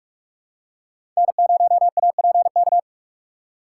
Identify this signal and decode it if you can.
Morse code, heard as a keyed tone, about 43 wpm, 700 Hz. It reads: N0APK
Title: USA Call Signs - 4x Character Spacing - 45wpm